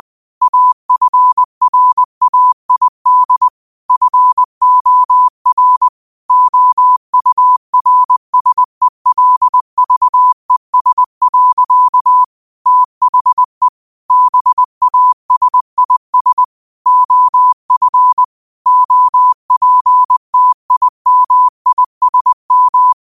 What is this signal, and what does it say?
Morse code, heard as a keyed tone, about 20 wpm, 1000 Hz. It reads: AFRAID FOR OURSELVES. THE BASIS OF OPTIMISM
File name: qotdfiles / 20wpm